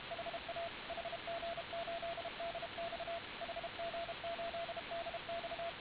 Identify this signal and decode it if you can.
Morse code, heard as a keyed tone, about 32 wpm, 680 Hz, with radio noise. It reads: HA HG8DX HG8DX